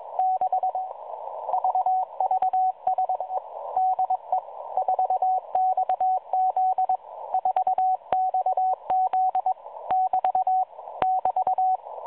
Morse code, heard as a keyed tone, about 21 wpm, 750 Hz, with radio noise. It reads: 6 VVSE DE 4XZ 4XZ <BT> <BT>
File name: 4XZ Shazepe